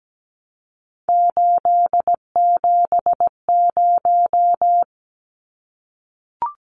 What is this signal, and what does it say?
Morse code, heard as a keyed tone, about 17 wpm, 700 Hz. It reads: 870
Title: Numbers - 3-Digit 17wpm